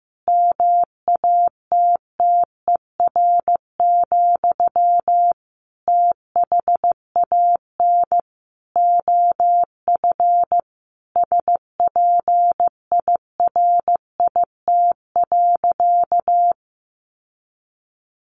Morse code, heard as a keyed tone, about 15 wpm, 700 Hz. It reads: MATTER, THAN OF SPIRIT.
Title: Sentences from Top 1400 Words 15wpm